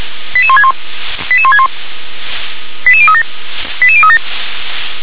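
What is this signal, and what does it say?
Five-tone selective call (ZVEI), heard as quick runs of five tones, tones of about 70 ms. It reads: 88161 88161 88037 88037
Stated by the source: ZVEI five-tone selective call 88037, received by SDR.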